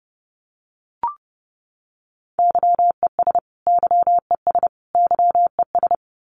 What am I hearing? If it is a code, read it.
Morse code, heard as a keyed tone, about 30 wpm, 700 Hz. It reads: YES YES YES
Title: Top 500 Words Repeated 30wpm